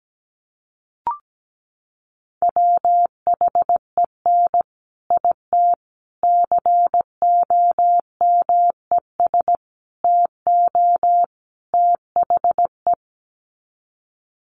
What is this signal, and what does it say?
Morse code, heard as a keyed tone, about 17 wpm, 700 Hz. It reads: WHEN IT COMES TO THE
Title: Sets of 5 Words 17wpm